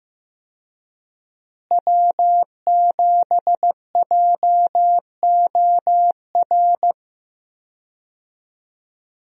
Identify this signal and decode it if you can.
Morse code, heard as a keyed tone, about 15 wpm, 700 Hz. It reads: W7JOR
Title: USA Call Signs - 5 Characters - Repeated 2x - 15wpm